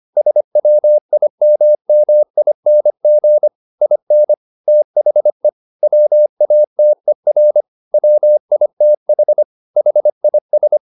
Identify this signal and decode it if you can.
Morse code, heard as a keyed tone, about 25 wpm, 600 Hz. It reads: SWIMMING IN THE WATER WITH HIS
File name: the old man and the sea part 1 25wpm